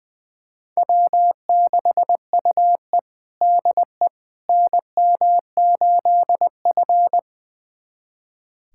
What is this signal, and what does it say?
Morse code, heard as a keyed tone, about 21 wpm, 700 Hz. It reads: W6UE DE NM8F
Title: USA Callsign DE Callsign - 4 Characters - 20wpm